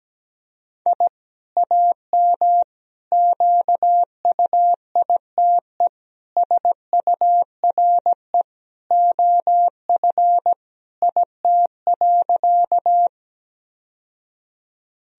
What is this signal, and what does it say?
Morse code, heard as a keyed tone, about 17 wpm, 700 Hz. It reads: I AM QUITE SURE OF IT.
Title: Sentences from Top 1000 Words - Review Part 2 - 17wpm